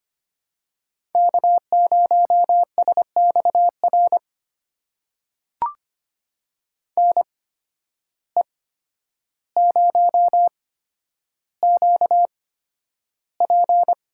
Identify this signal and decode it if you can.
Morse code, heard as a keyed tone, about 25 wpm, spaced as for 11 wpm, 700 Hz. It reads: K0SXR NE0QP
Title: USA Call Signs - 8x Character Spacing - 25wpm